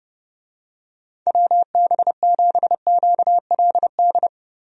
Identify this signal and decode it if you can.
Morse code, heard as a keyed tone, about 30 wpm, 700 Hz. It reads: WB7QLD
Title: USA Call Signs - 6 Characters - 30wpm